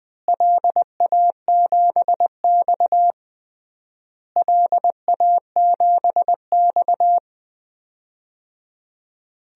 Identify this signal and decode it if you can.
Morse code, heard as a keyed tone, about 20 wpm, 700 Hz. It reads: LA7X LA7X
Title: DXCC-100 Call Signs - 4 Characters - Repeated 3x - 20wpm